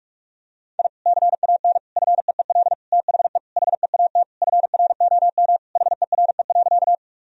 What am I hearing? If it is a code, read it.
Morse code, heard as a keyed tone, about 46 wpm, 700 Hz. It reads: I CAN FEEL THE HEAT FROM HERE.
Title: Sentences from Top 1000 Words - Review Part 3 - 45wpm